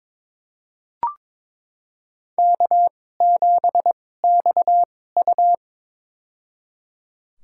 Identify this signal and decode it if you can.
Morse code, heard as a keyed tone, about 21 wpm, spaced as for 16 wpm, 700 Hz. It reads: K7XU
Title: USA Call Signs - 4 Characters - 2x Character Spacing - 22wpm